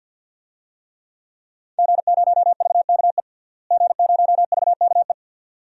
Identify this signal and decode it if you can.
Morse code, heard as a keyed tone, about 52 wpm, 700 Hz. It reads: G0VKE G0VKE
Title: DXCC-100 Call Signs - 5 Characters - Repeated 3x - 50wpm